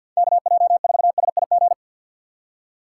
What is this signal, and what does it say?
Morse code, heard as a keyed tone, about 53 wpm, 700 Hz. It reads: KJ4SIG
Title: USA Call Signs - 6 Characters - 3x Character Spacing - 50wpm